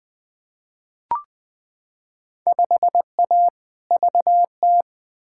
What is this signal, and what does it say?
Morse code, heard as a keyed tone, about 21 wpm, 700 Hz. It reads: 5A VT
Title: Field Day Exchanges Encore - 20wpm